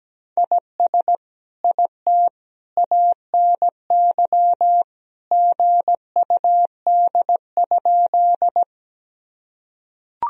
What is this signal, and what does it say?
Morse code, heard as a keyed tone, about 17 wpm, 700 Hz. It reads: IS IT ANY GUD?
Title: Sentences from Top 200 Words and Abbreviations 17wpm